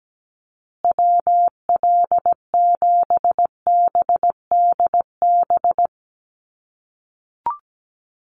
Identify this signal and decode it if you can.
Morse code, heard as a keyed tone, about 17 wpm, 700 Hz. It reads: WL7BDB
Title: USA Call Signs Repeated 2x 17wpm